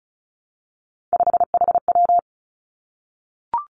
Morse code, heard as a keyed tone, about 36 wpm, 700 Hz. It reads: 5HW
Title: Three Challenging Letter-Number Combinations 35wpm